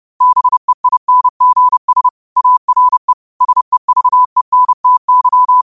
Morse code, heard as a keyed tone, about 30 wpm, 1000 Hz. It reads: BEINGS ARE SEVENTY
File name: qotdfiles / 30wpm